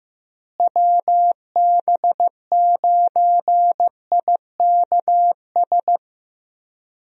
Morse code, heard as a keyed tone, about 15 wpm, 700 Hz. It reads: WB9IKS
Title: USA Call Signs - 6 Characters - Repeated 2x - 15wpm